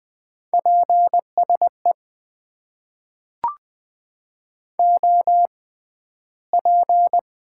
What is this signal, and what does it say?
Morse code, heard as a keyed tone, about 20 wpm, spaced as for 12 wpm, 700 Hz. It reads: PSE OP
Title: Top 90 QSO Elements - 6x Character Spacing - 20wpm